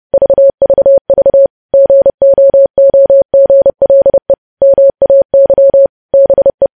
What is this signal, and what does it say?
Morse code, heard as a keyed tone, about 30 wpm, 550 Hz. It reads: VVV GOOGLE MAY BE
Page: news headlines in morse code at 30 wpm